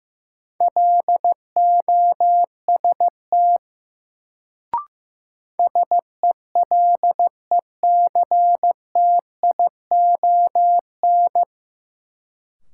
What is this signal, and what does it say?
Morse code, heard as a keyed tone, about 15 wpm, 700 Hz. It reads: LOST SELECTION
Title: Sentences from Top 1400 Words - New Word Drill Alt - 15wpm